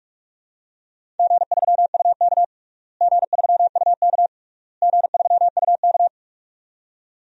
Morse code, heard as a keyed tone, about 46 wpm, 700 Hz. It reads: G3UK G3UK G3UK
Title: DXCC-100 Call Signs - 4 Characters - Repeated 3x - 45wpm